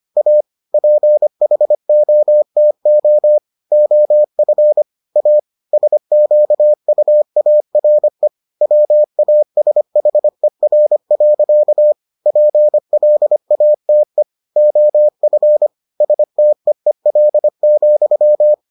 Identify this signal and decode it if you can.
Morse code, heard as a keyed tone, about 25 wpm, 600 Hz. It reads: A PHOTO OF A SQUARE WASHER. PLATE OF STEEL,